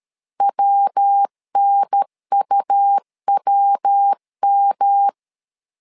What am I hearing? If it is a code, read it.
Morse code, heard as a keyed tone, about 13 wpm, 800 Hz. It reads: WNUWM
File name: LEZIONI / 3 MWRO / 3 MW 12wpm 1633